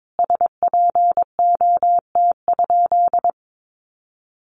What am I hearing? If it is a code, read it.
Morse code, heard as a keyed tone, about 22 wpm, 700 Hz. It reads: SPOT?
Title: Sentences from Top 1100 Words 22wpm